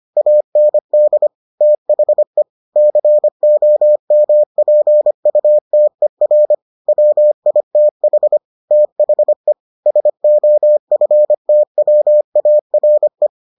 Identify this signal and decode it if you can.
Morse code, heard as a keyed tone, about 25 wpm, 600 Hz. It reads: AND THE COMPUTER WITH THE SOFTWARE